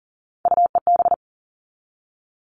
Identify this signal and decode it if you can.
Morse code, heard as a keyed tone, about 40 wpm, 700 Hz. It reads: UEB
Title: All Cut Numbers - 3-Digit 40wpm